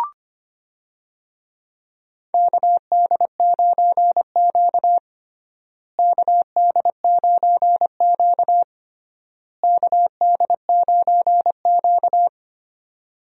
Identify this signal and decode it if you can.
Morse code, heard as a keyed tone, about 25 wpm, 700 Hz. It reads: KD9Q KD9Q KD9Q
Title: USA Call Signs - 4 Characters - Repeated 3x - 25wpm